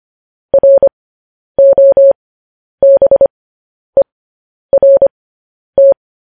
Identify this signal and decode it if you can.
Morse code, heard as a keyed tone, about 25 wpm, spaced as for 10 wpm, 550 Hz. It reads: ROBERT
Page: news-headlines-in-morse-code-at-10-wpm